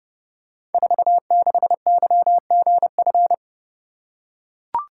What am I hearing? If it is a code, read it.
Morse code, heard as a keyed tone, about 30 wpm, 700 Hz. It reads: 46YGF
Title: Code Groups - 5 Letter:Numbers - 30wpm